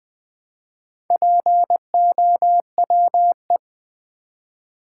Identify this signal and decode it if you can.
Morse code, heard as a keyed tone, about 20 wpm, 700 Hz. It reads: POWE
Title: Build A Word - 2 Letters At A Time - 8-Letter Words - 20wpm